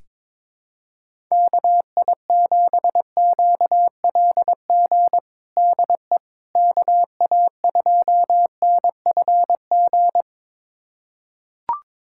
Morse code, heard as a keyed tone, about 22 wpm, 700 Hz. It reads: KI7QLG DE KA2NFG
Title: USA Callsign DE Callsign - 6 Characters - 22wpm